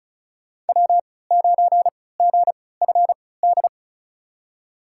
Morse code, heard as a keyed tone, about 34 wpm, spaced as for 20 wpm, 700 Hz. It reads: W9GFD
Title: USA Call Signs - 5 Characters - 3x Character Spacing - 35wpm